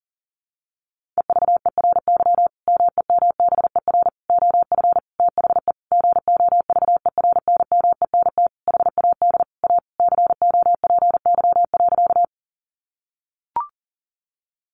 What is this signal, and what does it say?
Morse code, heard as a keyed tone, about 40 wpm, 700 Hz. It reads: EVERY MEMBER OF THE GOVERNMENT HAD A COPY.
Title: Sentences from Top 1400 Words 40wpm